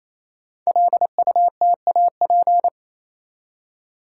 Morse code, heard as a keyed tone, about 27 wpm, 690 Hz. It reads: LUTAP
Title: Code Groups - 5 Letters - 28wpm